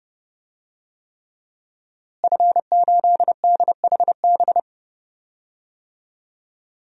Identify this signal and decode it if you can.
Morse code, heard as a keyed tone, about 31 wpm, 700 Hz. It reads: F8DHB
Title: DXCC-100 Call Signs - 5 Characters - Repeated 3x - 30wpm